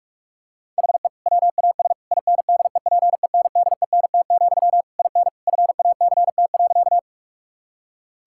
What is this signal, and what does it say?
Morse code, heard as a keyed tone, about 41 wpm, 700 Hz. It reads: HE WAS INDEPENDENT, IN FACT.
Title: Sentences from Top 1000 Words - Review Part 6 - 45wpm